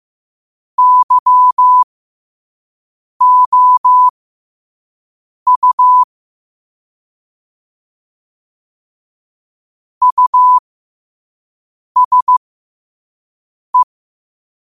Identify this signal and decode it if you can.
Morse code, heard as a keyed tone, about 15 wpm, spaced as for 5 wpm, 1000 Hz. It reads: YOU USE
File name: 5wpm